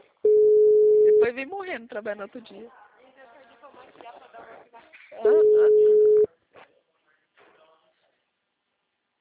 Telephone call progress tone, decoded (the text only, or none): ringback tone